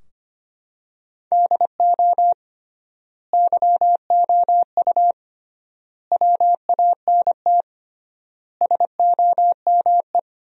Morse code, heard as a keyed tone, about 25 wpm, 700 Hz. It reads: DO YOU WANT SOME
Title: 3x Word-Spacing - Sentences from Top 300 Words - 25wpm